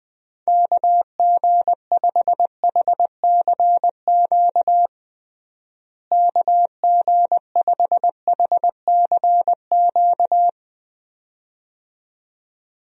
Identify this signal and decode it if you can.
Morse code, heard as a keyed tone, about 20 wpm, 700 Hz. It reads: KG5HCQ KG5HCQ
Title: USA Call Signs - 6 Characters - Repeated 2x - 20wpm